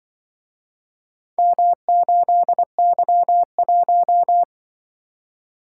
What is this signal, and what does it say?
Morse code, heard as a keyed tone, about 23 wpm, 700 Hz. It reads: M8Y1
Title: Speed-Racing - Four Letter-Number Combinations - 20wpm